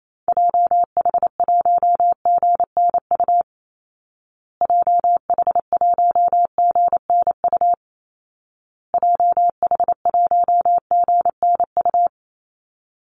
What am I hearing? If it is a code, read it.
Morse code, heard as a keyed tone, about 28 wpm, 700 Hz. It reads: JH1GNU JH1GNU JH1GNU
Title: Call Signs Repeated 2x - 28wpm